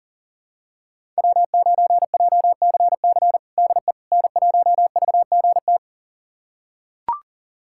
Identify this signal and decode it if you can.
Morse code, heard as a keyed tone, about 40 wpm, 700 Hz. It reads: W9JCC DE N1VGT
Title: USA Callsign DE Callsign - 5 Characters - 40wpm